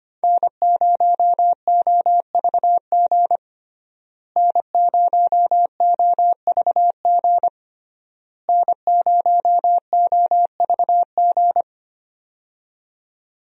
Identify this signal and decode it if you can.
Morse code, heard as a keyed tone, about 25 wpm, 700 Hz. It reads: N0OVG N0OVG N0OVG
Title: USA Call Signs - 5 Characters - Repeated 3x - 25wpm